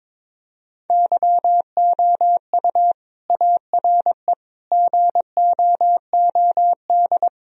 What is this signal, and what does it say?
Morse code, heard as a keyed tone, about 22 wpm, 700 Hz. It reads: YOU ARE GOOD
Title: Sentences from Top 1000 Words - Review Part 1 - 22wpm